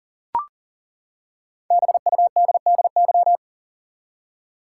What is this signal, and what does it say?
Morse code, heard as a keyed tone, about 40 wpm, 700 Hz. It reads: BUDDY
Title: Five Letter Words - 40wpm